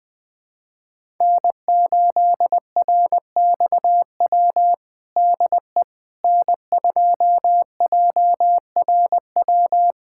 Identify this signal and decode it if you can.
Morse code, heard as a keyed tone, about 20 wpm, 700 Hz. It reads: N8RXW DE N2JRW
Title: USA Callsign DE Callsign - 5 Characters - 20wpm